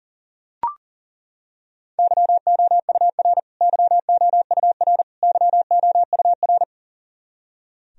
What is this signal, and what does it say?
Morse code, heard as a keyed tone, about 39 wpm, 700 Hz. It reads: YOUR YOUR YOUR
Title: Top 100 Words Repeated 40wpm